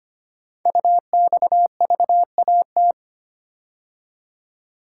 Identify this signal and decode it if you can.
Morse code, heard as a keyed tone, about 25 wpm, 700 Hz. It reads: UXVAT
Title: Code Groups - 5 Letters - 25wpm